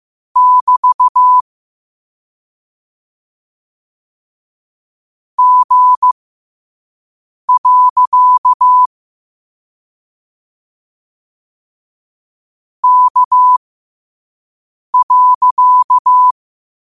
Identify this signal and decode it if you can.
Morse code, heard as a keyed tone, about 15 wpm, spaced as for 5 wpm, 1000 Hz. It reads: <BT> G. K.
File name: qotdfiles / 5wpm